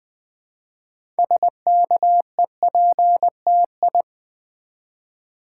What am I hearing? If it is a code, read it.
Morse code, heard as a keyed tone, about 20 wpm, 700 Hz. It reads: SKEPTI
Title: Build A Word - 3 Letters At A Time - 9-Letter Words - 20wpm